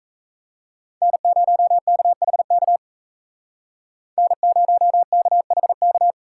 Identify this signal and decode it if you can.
Morse code, heard as a keyed tone, about 36 wpm, 700 Hz. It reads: N0KHK N0KHK